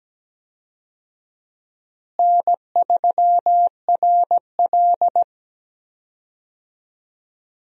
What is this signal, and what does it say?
Morse code, heard as a keyed tone, about 17 wpm, 700 Hz. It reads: N3RL
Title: USA Call Signs - 4 Characters - Repeated 3x - 17wpm